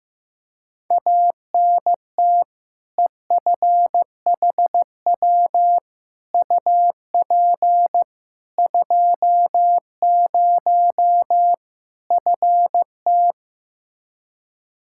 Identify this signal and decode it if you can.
Morse code, heard as a keyed tone, about 15 wpm, 700 Hz. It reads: ANT EFHW UP 20 FT
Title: Practice QSO - Exchange 2 - Info - 15wpm